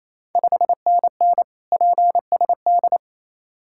Morse code, heard as a keyed tone, about 28 wpm, 690 Hz. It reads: 5NN PSD
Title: Ontario QSO Party Exchange 28wpm